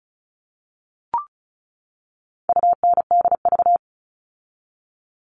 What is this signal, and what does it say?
Morse code, heard as a keyed tone, about 34 wpm, 700 Hz. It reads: UNDV
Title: All Cut Numbers - 4-Digit 35wpm